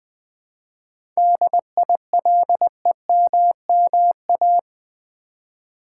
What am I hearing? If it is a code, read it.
Morse code, heard as a keyed tone, about 20 wpm, 700 Hz. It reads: DILEMMA
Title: 7 Letter Words - First 3 Letters Given - 20wpm